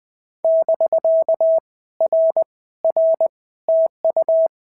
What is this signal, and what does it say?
Morse code, heard as a keyed tone, about 19 wpm, 650 Hz. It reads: <BK> R R TU
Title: POTA QSO Practice - 20wpm